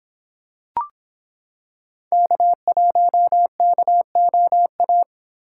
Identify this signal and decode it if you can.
Morse code, heard as a keyed tone, about 26 wpm, 700 Hz. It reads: K1KOA